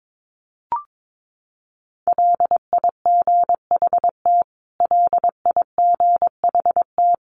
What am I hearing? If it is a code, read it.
Morse code, heard as a keyed tone, about 22 wpm, 700 Hz. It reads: LIGHT LIGHT
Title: Top 500 Words Repeated 22wpm